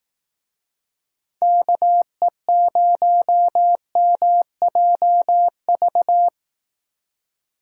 Morse code, heard as a keyed tone, about 18 wpm, 700 Hz. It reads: KE0MJV
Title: Speed Racing - USA Call Signs 15wpm